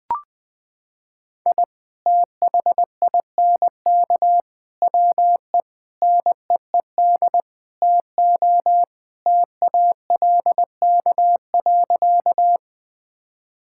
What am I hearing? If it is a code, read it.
Morse code, heard as a keyed tone, about 20 wpm, 700 Hz. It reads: I THINK WE NEED TO TALK.
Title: Sentences with 5-Word Phrases - 20wpm